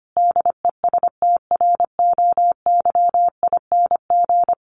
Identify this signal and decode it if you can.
Morse code, heard as a keyed tone, about 25 wpm, 700 Hz. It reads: DESTROYING
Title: A Sound Of Thunder 25wpm